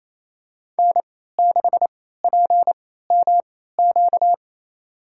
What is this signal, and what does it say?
Morse code, heard as a keyed tone, about 28 wpm, spaced as for 16 wpm, 700 Hz. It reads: N6PMQ
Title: USA Call Signs - 3x Character Spacing - 28wpm